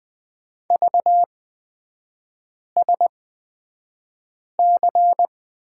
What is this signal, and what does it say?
Morse code, heard as a keyed tone, about 20 wpm, spaced as for 5 wpm, 700 Hz. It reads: VSC
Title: 224 Lesson 36e - BT - Three Characters 20wpm